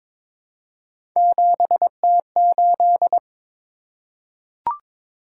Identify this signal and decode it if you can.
Morse code, heard as a keyed tone, about 22 wpm, 700 Hz. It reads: 7T8
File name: practice / 22wpm / Three Letter-Number Combinations - 22wpm